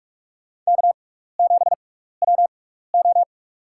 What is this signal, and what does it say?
Morse code, heard as a keyed tone, about 44 wpm, spaced as for 15 wpm, 700 Hz. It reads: K7WO
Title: USA Call Signs - 4 Characters - 6x Character Spacing - 45wpm